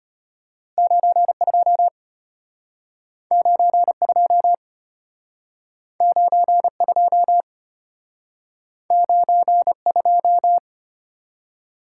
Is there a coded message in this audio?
Yes: Morse code from 0.8 to 10.6 s, heard as a keyed tone, about 32 wpm, 700 Hz. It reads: 92 92 92 92